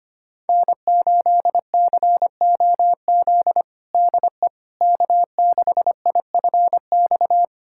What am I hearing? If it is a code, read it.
Morse code, heard as a keyed tone, about 25 wpm, 700 Hz. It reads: N8COZ DE K6IFX
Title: USA Callsign DE Callsign - 5 Characters - 25wpm